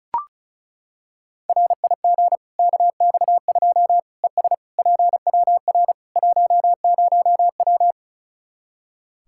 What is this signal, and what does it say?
Morse code, heard as a keyed tone, about 36 wpm, 700 Hz. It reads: RIG KX2 ES PWR 10W
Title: Practice QSO - Exchange 2 - Info - 35wpm